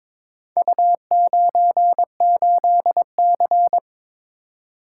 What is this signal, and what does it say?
Morse code, heard as a keyed tone, about 21 wpm, 700 Hz. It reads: U98C
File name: practice / 22wpm / Four Letter-Number Combinations - 22wpm